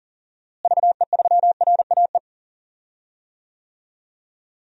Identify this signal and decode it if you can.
Morse code, heard as a keyed tone, about 40 wpm, 700 Hz. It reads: VE3RAE
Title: DXCC-100 Call Signs - 6 Characters - Repeated 2x - 40wpm